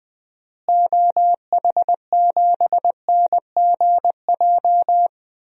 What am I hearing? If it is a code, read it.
Morse code, heard as a keyed tone, about 21 wpm, 700 Hz. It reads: OH7NGJ
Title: DXCC-100 Call Signs - 6 Characters - Repeated 3x - 20wpm